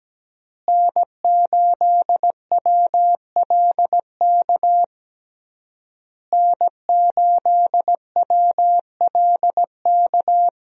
Morse code, heard as a keyed tone, about 17 wpm, 700 Hz. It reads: N8WLK N8WLK
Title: USA Call Signs - 5 Characters - Repeated 3x - 17wpm